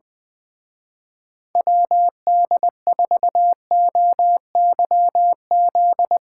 Morse code, heard as a keyed tone, about 22 wpm, 700 Hz. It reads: WD4OYZ
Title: Speed Racing - USA 2x3 Call Signs 20wpm